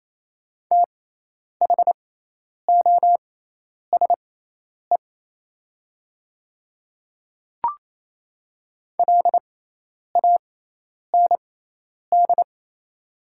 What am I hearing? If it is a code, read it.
Morse code, heard as a keyed tone, about 28 wpm, spaced as for 10 wpm, 700 Hz. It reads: THOSE LAND
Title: 6x Character Spacing - Top 300 Words - 28wpm